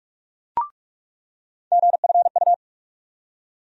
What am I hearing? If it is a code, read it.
Morse code, heard as a keyed tone, about 46 wpm, 700 Hz. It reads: GFU
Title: USA Call Sign Postfixes - 3 Letters 45wpm